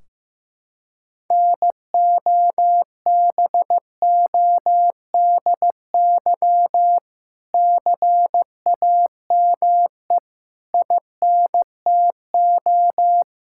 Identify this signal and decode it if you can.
Morse code, heard as a keyed tone, about 15 wpm, 700 Hz. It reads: NOBODY CAME INTO
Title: Sentences from Top 1600 Words 15wpm